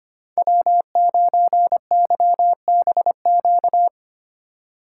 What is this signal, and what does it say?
Morse code, heard as a keyed tone, about 25 wpm, 700 Hz. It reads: W9YBQ
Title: USA Call Signs - 5 Characters - 2x Character Spacing - 25wpm